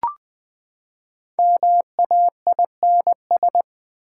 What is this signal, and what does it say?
Morse code, heard as a keyed tone, about 20 wpm, 700 Hz. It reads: MAINS
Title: Build A Word - 5 Letters At A Time - 10-Letter Words - 20wpm